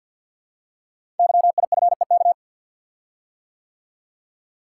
Morse code, heard as a keyed tone, about 52 wpm, 700 Hz. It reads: YIFEK
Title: Code Groups - 5 Letters - 50wpm